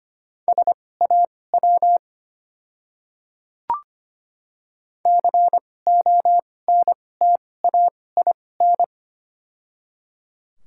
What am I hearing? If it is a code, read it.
Morse code, heard as a keyed tone, about 25 wpm, spaced as for 18 wpm, 700 Hz. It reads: SAW CONTAIN
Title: 2x Character Spacing - Top 500 Words - 25wpm